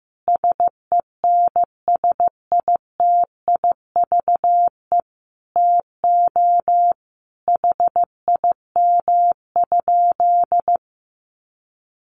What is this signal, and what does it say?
Morse code, heard as a keyed tone, about 15 wpm, 700 Hz. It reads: SENSITIVE TO HIM?
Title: Sentences from Top 1400 Words 15wpm